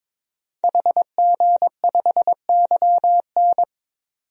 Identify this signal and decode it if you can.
Morse code, heard as a keyed tone, about 22 wpm, 700 Hz. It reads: HG5YN
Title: DXCC-100 Call Signs - 5 Characters - 22wpm